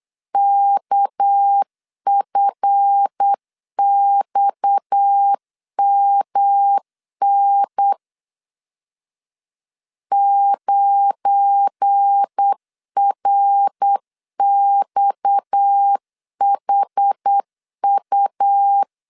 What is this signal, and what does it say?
Morse code, heard as a keyed tone, about 8 wpm, 800 Hz. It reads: KFXMN 9RXHU
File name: LEZIONI / 7 GF9Z / 7 GF9 8wpm 3545